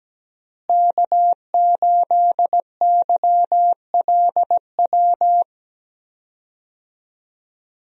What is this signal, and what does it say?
Morse code, heard as a keyed tone, about 17 wpm, 700 Hz. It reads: K8YLW
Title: USA Call Signs - 5 Characters - Repeated 3x - 17wpm